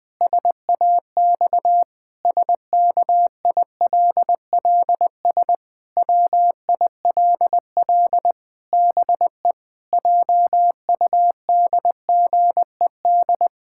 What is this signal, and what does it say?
Morse code, heard as a keyed tone, about 21 wpm, 700 Hz. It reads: SAX SKILLS WILL BE JUDGED
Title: Pangram Challenge v2 - 20wpm